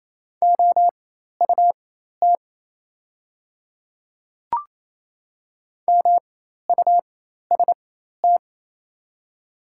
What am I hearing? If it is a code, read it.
Morse code, heard as a keyed tone, about 28 wpm, spaced as for 13 wpm, 700 Hz. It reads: OUT MUST
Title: 4x Character Spacing - Top 200 Words - 28wpm